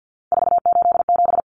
Morse code, heard as a keyed tone, about 50 wpm, 700 Hz. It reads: 487
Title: Numbers - 3-Digit 50wpm